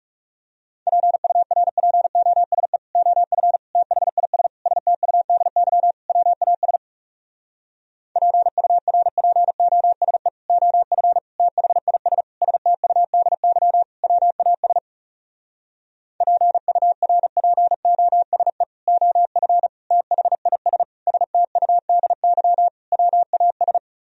Morse code, heard as a keyed tone, about 40 wpm, 700 Hz. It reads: PURPOSE OF THIS STUDY WAS PURPOSE OF THIS STUDY WAS PURPOSE OF THIS STUDY WAS